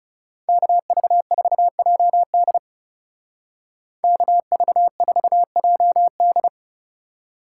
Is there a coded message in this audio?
Yes: Morse code from 0.5 to 6.5 s, heard as a keyed tone, about 39 wpm, 700 Hz. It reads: KV4JD KV4JD